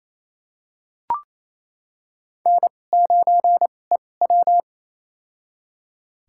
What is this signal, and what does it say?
Morse code, heard as a keyed tone, about 27 wpm, spaced as for 20 wpm, 700 Hz. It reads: N9EW
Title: USA Call Signs - 2x Character Spacing - 28wpm